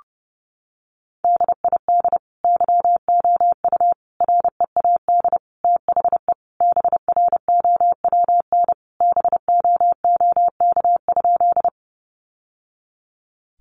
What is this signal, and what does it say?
Morse code, heard as a keyed tone, about 30 wpm, 700 Hz. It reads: DID YOU READ THE BROWN BOOK?
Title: Sentences from Top 1200 Words 30wpm